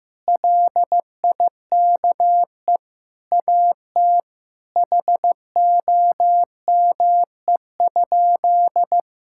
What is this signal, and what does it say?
Morse code, heard as a keyed tone, about 15 wpm, 700 Hz. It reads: LIKE AT HOME?
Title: Sentences from Top 1000 Words - Review Part 5 - 15wpm